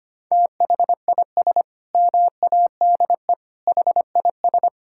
Morse code, heard as a keyed tone, about 25 wpm, 700 Hz. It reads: THIS MADE HIS
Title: Sentences from Top 1600 Words 25wpm